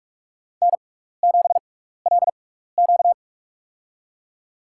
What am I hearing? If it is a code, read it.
Morse code, heard as a keyed tone, about 42 wpm, spaced as for 15 wpm, 700 Hz. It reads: N7LQ
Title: USA Call Signs - 4 Characters - 6x Character Spacing - 45wpm